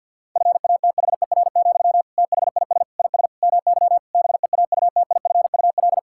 Morse code, heard as a keyed tone, about 50 wpm, 700 Hz. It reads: FATHER, THIS IS MY BEAUTIFUL
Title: Sentences from Top 1000 Words - Review Part 6 - 50wpm